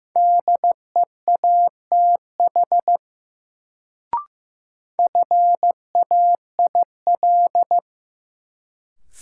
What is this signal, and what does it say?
Morse code, heard as a keyed tone, about 16 wpm, 700 Hz. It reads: DEATH FAIL
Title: Sentences from Top 1100 Words - New Word Drill - 15wpm